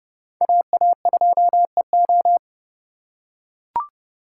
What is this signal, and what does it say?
Morse code, heard as a keyed tone, about 30 wpm, 700 Hz. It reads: AA2EO